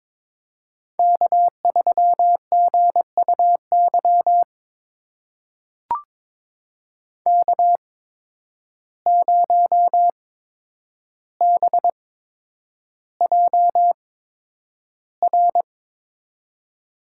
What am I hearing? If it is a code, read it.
Morse code, heard as a keyed tone, about 23 wpm, spaced as for 10 wpm, 700 Hz. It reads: K3GUY K0BJR
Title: USA Call Signs - 5 Characters - 8x Character Spacing - 22wpm